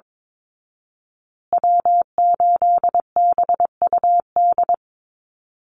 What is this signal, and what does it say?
Morse code, heard as a keyed tone, about 22 wpm, 700 Hz. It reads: W8BUD
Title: Call Signs Repeated 3x - 22wpm